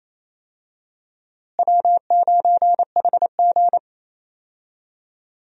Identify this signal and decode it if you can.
Morse code, heard as a keyed tone, about 28 wpm, 700 Hz. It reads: W9HG
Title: USA Call Signs - 4 Characters - Repeated 3x - 28wpm